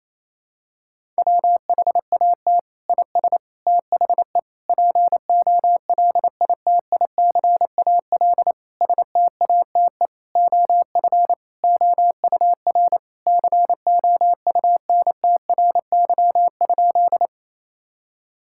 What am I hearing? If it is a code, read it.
Morse code, heard as a keyed tone, about 28 wpm, 700 Hz. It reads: WHAT IS THE POLITICAL STATE OF OUR COUNTRY?
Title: Sentences from Top 1000 Words - Review Part 5 - 28wpm